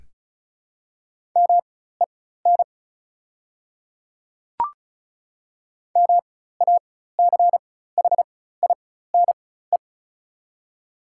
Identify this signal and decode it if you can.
Morse code, heard as a keyed tone, about 35 wpm, spaced as for 16 wpm, 690 Hz. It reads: MEN MACHINE
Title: 4x Character Spacing - Top 500 Words - 35wpm